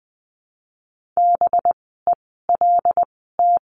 Morse code, heard as a keyed tone, about 20 wpm, spaced as for 14 wpm, 700 Hz. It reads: BELT
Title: 185 Lesson 30f - Letter b - Words 20wpm